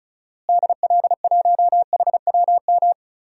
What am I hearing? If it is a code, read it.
Morse code, heard as a keyed tone, about 36 wpm, 700 Hz. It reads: DL1HWM
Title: DXCC-100 Call Signs - 6 Characters - Repeated 3x - 35wpm